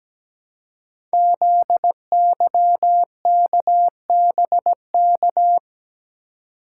Morse code, heard as a keyed tone, about 17 wpm, 700 Hz. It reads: ZYKBK
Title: Code Groups - 5 Letters - 17wpm